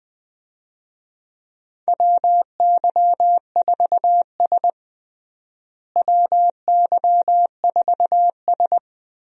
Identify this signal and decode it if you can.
Morse code, heard as a keyed tone, about 21 wpm, 700 Hz. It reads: WY4S WY4S
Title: USA Call Signs - 4 Characters - Repeated 2x - 20wpm